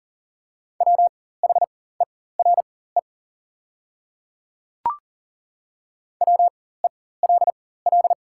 Morse code, heard as a keyed tone, about 40 wpm, spaced as for 19 wpm, 690 Hz. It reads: WHERE WELL
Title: 4x Character Spacing - Top 200 Words - 40wpm